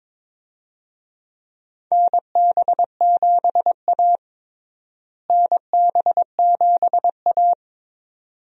Morse code, heard as a keyed tone, about 22 wpm, 700 Hz. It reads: NB7A NB7A
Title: USA Call Signs - 4 Characters - Repeated 2x - 22wpm